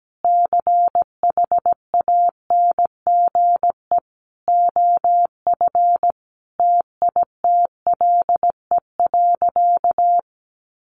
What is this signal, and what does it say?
Morse code, heard as a keyed tone, about 17 wpm, 700 Hz. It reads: CHANGE OF TITLE.
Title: Sentences from Top 1200 Words 17wpm